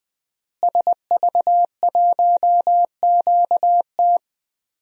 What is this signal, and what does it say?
Morse code, heard as a keyed tone, about 20 wpm, 700 Hz. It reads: SV1QT
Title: DXCC-100 Call Signs - 5 Characters - Repeated 3x - 20wpm